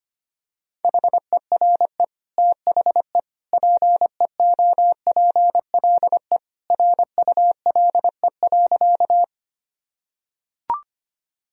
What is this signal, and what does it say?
Morse code, heard as a keyed tone, about 25 wpm, 700 Hz. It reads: HERE THE PEOPLE RULE.
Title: Sentences from Top 1000 Words - Review Part 2 - 25wpm